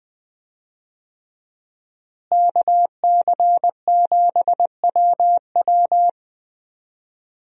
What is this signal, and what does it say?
Morse code, heard as a keyed tone, about 20 wpm, 700 Hz. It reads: KC7WW
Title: USA Call Signs - 5 Characters - Repeated 3x - 20wpm